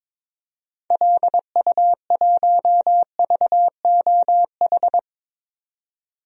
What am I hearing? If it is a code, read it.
Morse code, heard as a keyed tone, about 22 wpm, 700 Hz. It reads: LU1VOH
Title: DXCC-100 Call Signs - 6 Characters - 22wpm